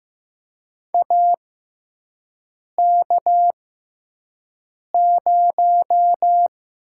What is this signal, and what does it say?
Morse code, heard as a keyed tone, about 15 wpm, spaced as for 5 wpm, 700 Hz. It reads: AK0
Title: USA Call Signs - 6x Character Spacing - 15wpm